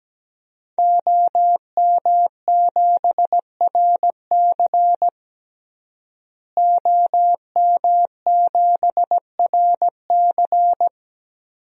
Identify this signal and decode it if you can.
Morse code, heard as a keyed tone, about 17 wpm, 700 Hz. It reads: OM7RC OM7RC
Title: DXCC-100 Call Signs - 5 Characters - Repeated 2x - 17wpm